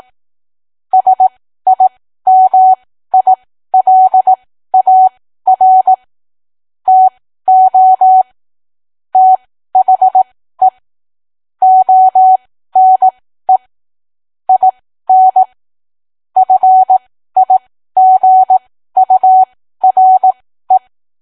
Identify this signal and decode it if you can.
Morse code, heard as a keyed tone, about 18 wpm, spaced as for 13 wpm, 750 Hz. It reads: SIMILAR TO THE ONE IN FIGURE